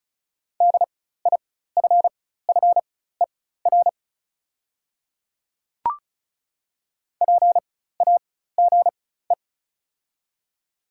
Morse code, heard as a keyed tone, about 36 wpm, spaced as for 16 wpm, 700 Hz. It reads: DIFFER PAGE
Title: 4x Character Spacing - Top 300 Words - 35wpm